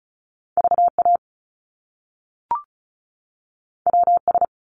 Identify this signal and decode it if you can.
Morse code, heard as a keyed tone, about 36 wpm, 700 Hz. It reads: VA WS
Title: Top Two Letter Combinations in Words 35wpm